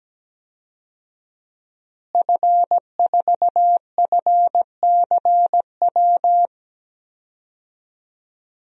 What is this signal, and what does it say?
Morse code, heard as a keyed tone, about 17 wpm, 700 Hz. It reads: F4FCW
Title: DXCC-100 Call Signs - 5 Characters - Repeated 3x - 17wpm